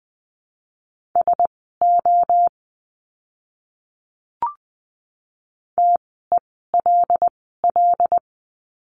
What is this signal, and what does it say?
Morse code, heard as a keyed tone, about 20 wpm, spaced as for 15 wpm, 700 Hz. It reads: SO TELL
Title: 230 Lesson 37f - Letter x - Words 20wpm